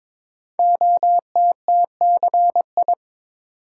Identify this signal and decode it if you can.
Morse code, heard as a keyed tone, about 22 wpm, 700 Hz. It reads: OTTCI
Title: Code Groups - 5 Letters - 22wpm